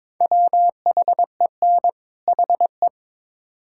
Morse code, heard as a keyed tone, about 22 wpm, 700 Hz. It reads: WHEN HE
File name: Build A Phrase - 2-Word Phrases - 22wpm